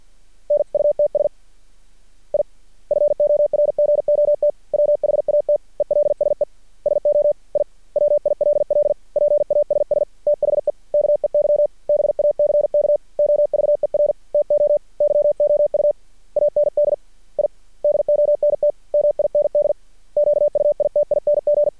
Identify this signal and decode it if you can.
Morse code, heard as a keyed tone, about 53 wpm, 590 Hz. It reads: NUTS I FORGOT WHAT ELSE SO I WILL PASS THE KEY BACK OVER TO YOU AND I DONT MIND QUITING
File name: archived / CRSnet